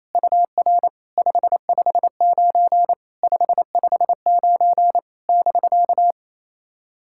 Morse code, heard as a keyed tone, about 28 wpm, 690 Hz. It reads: UR 559 559 <BK>
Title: SOTA QSO Practice 28wpm